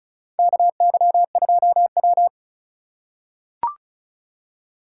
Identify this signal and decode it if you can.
Morse code, heard as a keyed tone, about 36 wpm, 700 Hz. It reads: KY2W
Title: USA Call Signs - 4 Characters - 3x Character Spacing - 35wpm